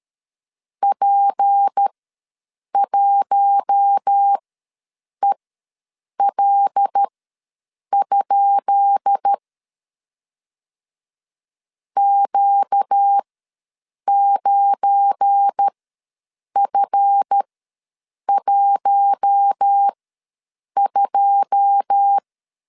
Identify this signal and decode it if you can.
Morse code, heard as a keyed tone, about 13 wpm, spaced as for 7 wpm, 800 Hz. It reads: P1EL? Q9F12
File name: LEZIONI / 10 SIGNS / 10B INT ws 12wpm 3527